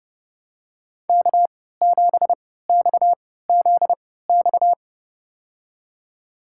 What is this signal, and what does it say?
Morse code, heard as a keyed tone, about 32 wpm, spaced as for 17 wpm, 700 Hz. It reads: K7XZX
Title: USA Call Signs - 5 Characters - 3x Character Spacing - 30wpm